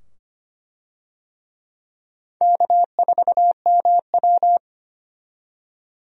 Morse code, heard as a keyed tone, about 25 wpm, 700 Hz. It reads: K4MW
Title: USA Call Signs - 4 Characters - 25wpm